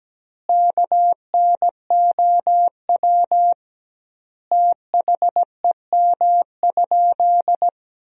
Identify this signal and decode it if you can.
Morse code, heard as a keyed tone, about 17 wpm, 700 Hz. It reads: KNOW THEM?
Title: 2x Word-Spacing - Sentences from Top 100 Words - 17wpm